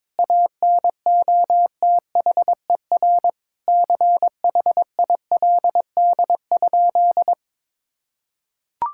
Morse code, heard as a keyed tone, about 22 wpm, 700 Hz. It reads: ANOTHER CHILD?
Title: Sentences from Top 1600 Words 22wpm